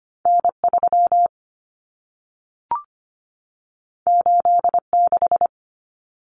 Morse code, heard as a keyed tone, about 25 wpm, 700 Hz. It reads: N3 86
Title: Common Cut Numbers - 2-Digit 25wpm